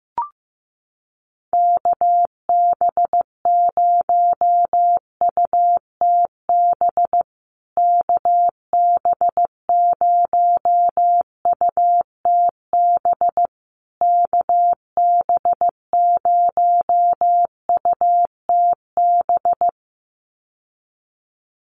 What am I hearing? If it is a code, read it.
Morse code, heard as a keyed tone, about 15 wpm, 700 Hz. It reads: KB0UTB KB0UTB KB0UTB
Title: USA Call Signs Repeated 3x 15wpm